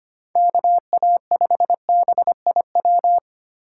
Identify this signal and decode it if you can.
Morse code, heard as a keyed tone, about 25 wpm, 700 Hz. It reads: KA5BIW
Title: USA Call Signs - 6 Characters - 2x Character Spacing - 25wpm